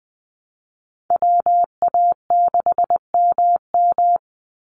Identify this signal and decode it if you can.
Morse code, heard as a keyed tone, about 20 wpm, 700 Hz. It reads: WA6MM
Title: NA SOTA Activator Callsigns 20wpm